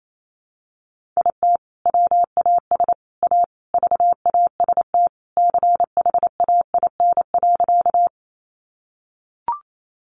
Morse code, heard as a keyed tone, about 28 wpm, 700 Hz. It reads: IT WAS A VAST CHAIN.
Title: Sentences from Top 1500 Words 28wpm